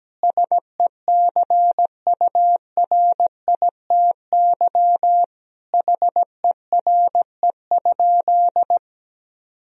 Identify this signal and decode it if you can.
Morse code, heard as a keyed tone, about 17 wpm, 700 Hz. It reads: SECURITY HERE?
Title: Sentences from Top 1000 Words - Review Part 4 - 17wpm